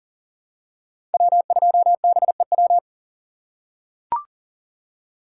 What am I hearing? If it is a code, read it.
Morse code, heard as a keyed tone, about 40 wpm, 700 Hz. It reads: W2BEW
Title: USA Call Signs - 5 Characters - 3x Character Spacing - 40wpm